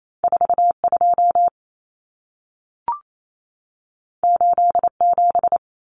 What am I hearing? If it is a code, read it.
Morse code, heard as a keyed tone, about 28 wpm, 700 Hz. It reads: 42 87
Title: Common Cut Numbers - 2-Digit 28wpm